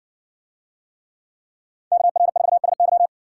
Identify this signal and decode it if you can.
Morse code, heard as a keyed tone, about 56 wpm, 700 Hz. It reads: KR4SY